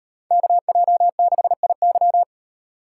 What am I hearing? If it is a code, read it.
Morse code, heard as a keyed tone, about 36 wpm, 700 Hz. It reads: KJ6IY